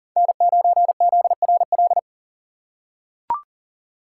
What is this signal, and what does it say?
Morse code, heard as a keyed tone, about 40 wpm, 700 Hz. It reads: N9ZRL
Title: USA Call Signs - 3x Character Spacing - 40wpm